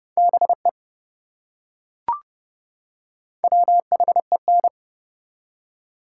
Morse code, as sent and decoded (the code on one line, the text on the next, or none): -... / .-- .... -.
BE WHEN